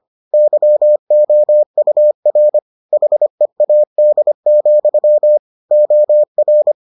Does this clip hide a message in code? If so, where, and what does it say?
Yes: Morse code from 0.3 to 6.7 s, heard as a keyed tone, about 25 wpm, 600 Hz. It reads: YOUR HEAD, OR